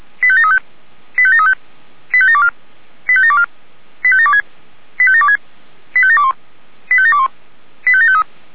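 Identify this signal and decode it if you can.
Five-tone selective call (ZVEI), heard as quick runs of five tones, tones of about 70 ms. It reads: 85735 85735 85724 85724 85726 85726 85721 85721 85753
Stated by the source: ZVEI five-tone selective call received by SDR.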